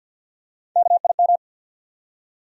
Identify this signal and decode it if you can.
Morse code, heard as a keyed tone, about 52 wpm, 700 Hz. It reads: KIM
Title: USA Common Names - 50wpm